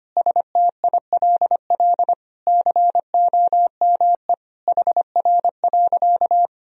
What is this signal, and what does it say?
Morse code, heard as a keyed tone, about 25 wpm, 700 Hz. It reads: STILL COME HR.
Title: Sentences from Top 300 Words and Abbreviations 25wpm